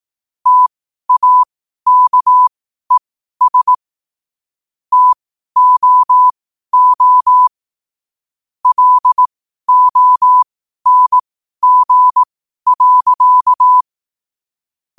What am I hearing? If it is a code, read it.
Morse code, heard as a keyed tone, about 18 wpm, spaced as for 12 wpm, 1000 Hz. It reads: TAKES TOO LONG.
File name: qotdfiles / 12wpm